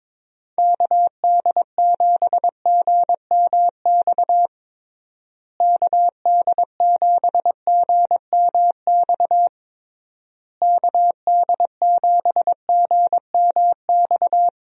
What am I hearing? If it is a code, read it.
Morse code, heard as a keyed tone, about 22 wpm, 700 Hz. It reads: KD7GMX KD7GMX KD7GMX
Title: USA Call Signs - 6 Characters - Repeated 3x - 22wpm